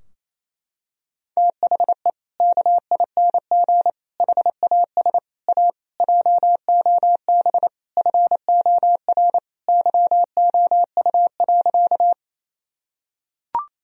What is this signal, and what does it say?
Morse code, heard as a keyed tone, about 28 wpm, 700 Hz. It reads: THE KING HAS A JOB FOR YOU.
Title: Sentences from Top 1000 Words - Review Part 3 - 28wpm